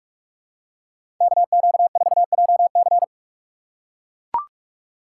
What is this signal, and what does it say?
Morse code, heard as a keyed tone, about 41 wpm, 700 Hz. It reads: KQ4JC
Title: USA Call Signs - 5 Characters - 2x Character Spacing - 45wpm